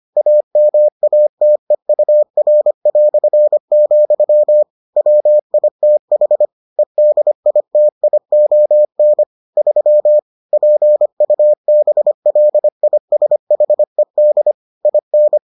Morse code, heard as a keyed tone, about 25 wpm, 600 Hz. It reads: AMATEUR", WITH EDITION 3 PUBLISHED IN